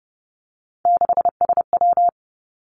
Morse code, heard as a keyed tone, about 30 wpm, 700 Hz. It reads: 6SW
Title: Three Challenging Letter-Number Combinations 30wpm